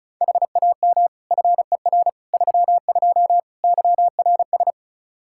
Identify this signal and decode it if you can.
Morse code, heard as a keyed tone, about 36 wpm, 700 Hz. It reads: HAM FER 32 YRS
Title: Practice QSO - Exchange 3 - More Info - 35wpm